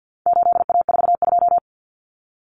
Morse code, heard as a keyed tone, about 50 wpm, 700 Hz. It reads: 8A42
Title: Common Cut Numbers - 4-Digit 50wpm